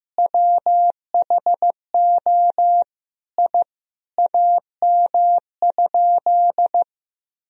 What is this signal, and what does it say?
Morse code, heard as a keyed tone, about 15 wpm, 700 Hz. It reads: WHO I AM?
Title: Sentences from Top 1000 Words - Review Part 2 - 15wpm